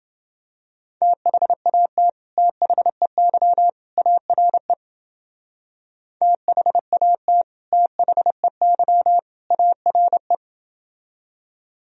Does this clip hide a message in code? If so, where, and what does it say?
Yes: Morse code from 1.0 to 10.3 s, heard as a keyed tone, about 29 wpm, 700 Hz. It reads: THAT THEY ARE THAT THEY ARE